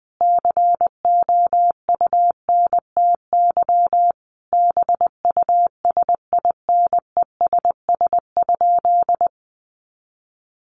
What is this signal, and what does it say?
Morse code, heard as a keyed tone, about 19 wpm, 700 Hz. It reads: COUNTY BUSINESS?
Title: Sentences from Top 1200 Words 20wpm